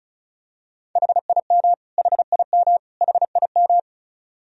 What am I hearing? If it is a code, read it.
Morse code, heard as a keyed tone, about 34 wpm, 700 Hz. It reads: HIM HIM HIM
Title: Top 100 Words Repeated 35wpm